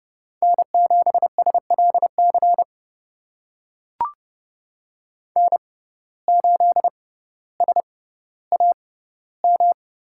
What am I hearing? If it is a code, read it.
Morse code, heard as a keyed tone, about 30 wpm, spaced as for 15 wpm, 700 Hz. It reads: N7SLC N8SAM
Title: USA Call Signs - 5 Characters - 6x Character Spacing - 30wpm